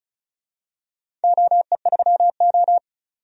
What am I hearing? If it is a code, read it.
Morse code, heard as a keyed tone, about 36 wpm, 700 Hz. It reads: OE3O
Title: DXCC-100 Call Signs - 4 Characters - 35wpm